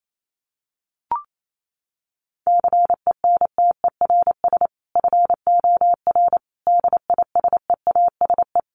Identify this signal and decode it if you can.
Morse code, heard as a keyed tone, about 28 wpm, 700 Hz. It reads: CENTERS FOR DISEASE
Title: Sets of 5 Words Encore2 28wpm